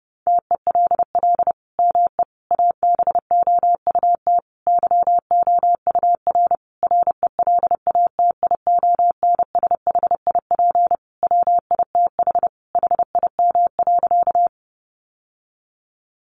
Morse code, heard as a keyed tone, about 30 wpm, 700 Hz. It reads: TELL ME ABOUT YOUR RELATIONSHIP WITH HIM.
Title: Sentences from Top 1100 Words 30wpm